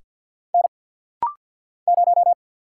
Morse code, heard as a keyed tone, about 49 wpm, 690 Hz. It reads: N 0
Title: Single Letter-Number - Rapid-Fire 50wpm